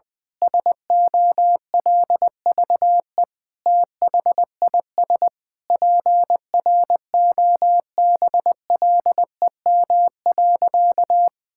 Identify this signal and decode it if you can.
Morse code, heard as a keyed tone, about 21 wpm, 700 Hz. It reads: SOLVE THIS PROBLEM.
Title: Sentences with 7-Word Phrases - Spoken Beginning - 20wpm